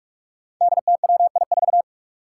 Morse code, heard as a keyed tone, about 44 wpm, 700 Hz. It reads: DTWI4
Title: Five Challenging Letter-Number Combinations 45wpm